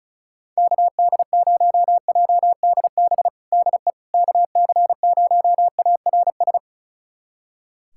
Morse code, heard as a keyed tone, about 36 wpm, 700 Hz. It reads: KD0JDB DE KC0ARS
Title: USA Callsign DE Callsign - 6 Characters - 35wpm